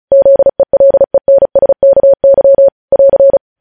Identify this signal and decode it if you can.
Morse code, heard as a keyed tone, about 36 wpm, 550 Hz. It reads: ZELENSKY <AR>
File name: podcast / News headlines-1x-35x35